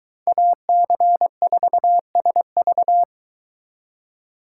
Morse code, heard as a keyed tone, about 23 wpm, 700 Hz. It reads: AC4SV